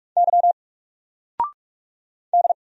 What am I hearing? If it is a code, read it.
Morse code, heard as a keyed tone, about 44 wpm, 700 Hz. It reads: Y D
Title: Single Letter-Number 45wpm